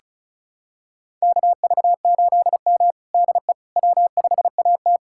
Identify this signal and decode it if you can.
Morse code, heard as a keyed tone, about 34 wpm, 690 Hz. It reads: KV8M DE W5AT
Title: USA Callsign DE Callsign - 4 Characters - 35wpm